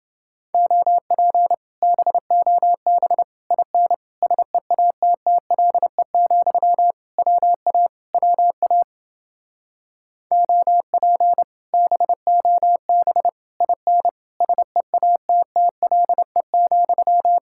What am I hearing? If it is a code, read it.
Morse code, heard as a keyed tone, about 30 wpm, 700 Hz. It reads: OP BOB IN SEATTLE, WA WA OP BOB IN SEATTLE,